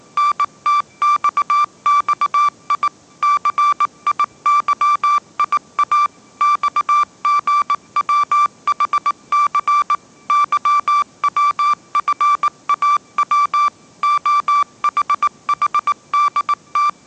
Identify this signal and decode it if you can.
Morse code, heard as a keyed tone, about 19 wpm, 1170 Hz. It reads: NTXXI CIYIA XGWHC YWFAW OHHDT